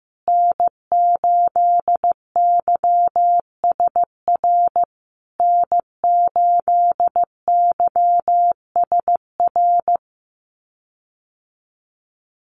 Morse code, heard as a keyed tone, about 15 wpm, 700 Hz. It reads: N8YSR N8YSR
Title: USA Call Signs Repeated 3x 15wpm